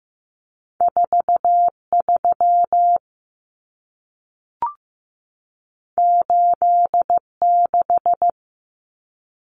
Morse code, heard as a keyed tone, about 15 wpm, 700 Hz. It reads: 43 86
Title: Common Cut Numbers - 2-Digit 15wpm